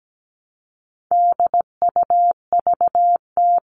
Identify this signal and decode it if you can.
Morse code, heard as a keyed tone, about 17 wpm, 700 Hz. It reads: DUVT
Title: All Cut Numbers - 4-Digit 17wpm